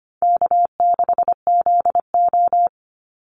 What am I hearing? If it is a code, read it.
Morse code, heard as a keyed tone, about 25 wpm, 700 Hz. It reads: K6ZO
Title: Call Signs - 25wpm